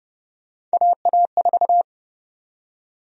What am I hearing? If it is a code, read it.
Morse code, heard as a keyed tone, about 30 wpm, 700 Hz. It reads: AA4
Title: USA Call Sign Prefixes - Repeated 2x - 30wpm